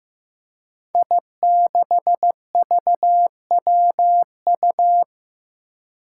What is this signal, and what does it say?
Morse code, heard as a keyed tone, about 15 wpm, 700 Hz. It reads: I6VWU
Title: Five Challenging Letter-Number Combinations 15wpm